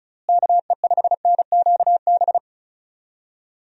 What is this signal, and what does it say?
Morse code, heard as a keyed tone, about 36 wpm, 700 Hz. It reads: KE5NQB
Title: USA Call Signs - 6 Characters - Repeated 2x - 35wpm